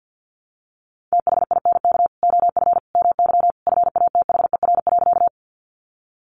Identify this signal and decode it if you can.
Morse code, heard as a keyed tone, about 50 wpm, 700 Hz. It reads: THINK OF MY FATHER.
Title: Sentences from Top 300 Words 50wpm